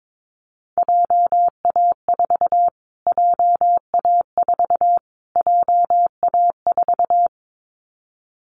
Morse code, heard as a keyed tone, about 22 wpm, 700 Hz. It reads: JA4 JA4 JA4
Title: Callsign Prefixes for Top 25 DXCC - Repeated 3x - 22wpm